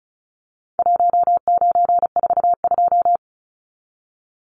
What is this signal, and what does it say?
Morse code, heard as a keyed tone, about 34 wpm, 700 Hz. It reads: 1942
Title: Numbers - Contest Style - 4-Digit 35wpm